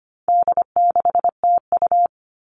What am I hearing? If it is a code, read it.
Morse code, heard as a keyed tone, about 25 wpm, 700 Hz. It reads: D6TU
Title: All Cut Numbers - 4-Digit 25wpm